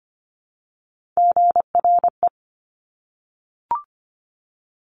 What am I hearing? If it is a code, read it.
Morse code, heard as a keyed tone, about 25 wpm, 700 Hz. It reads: GRE
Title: Top Three Letter Combinations in Words 25wpm